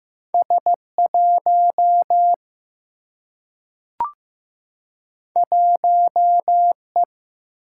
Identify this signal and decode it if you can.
Morse code, heard as a keyed tone, about 15 wpm, 700 Hz. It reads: S1 1E
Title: Single-Letter Single-Number 15wpm